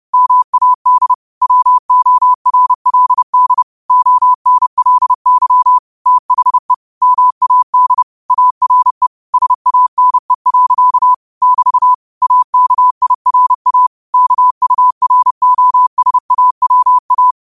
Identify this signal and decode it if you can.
Morse code, heard as a keyed tone, about 30 wpm, 1000 Hz. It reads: MAD WORLD ONLY THE MAD ARE SANE. <BT> AKIRA KUROSAWA